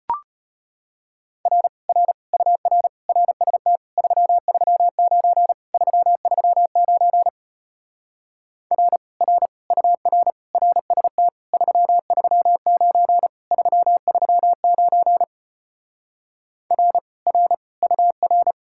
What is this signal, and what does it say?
Morse code, heard as a keyed tone, about 33 wpm, 700 Hz. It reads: R R UR RST 339 339 R R UR RST 339 339 R R UR